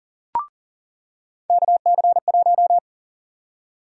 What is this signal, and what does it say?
Morse code, heard as a keyed tone, about 40 wpm, 700 Hz. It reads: KC1
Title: USA Call Sign Prefixes - 40wpm